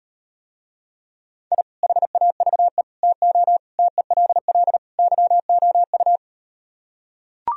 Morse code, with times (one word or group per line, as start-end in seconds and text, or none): 1.5-1.6 I
1.8-2.8 HAVE
3.0-3.6 TO
3.8-4.8 TELL
5.0-6.2 YOU